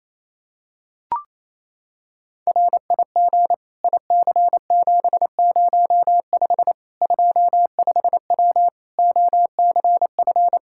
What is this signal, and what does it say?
Morse code, heard as a keyed tone, about 28 wpm, 700 Hz. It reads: RIG IC705 25W OCF
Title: K1USN SST QSO Practice 28wpm